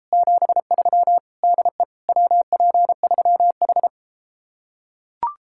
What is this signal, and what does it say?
Morse code, heard as a keyed tone, about 33 wpm, 700 Hz. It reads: 73 DE WP3H